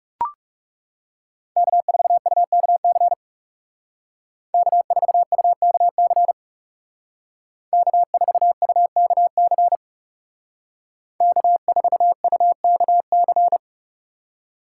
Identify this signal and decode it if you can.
Morse code, heard as a keyed tone, about 37 wpm, 700 Hz. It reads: K4UKC K4UKC K4UKC K4UKC
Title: Speed Racing - USA 1x3 Call Signs 30wpm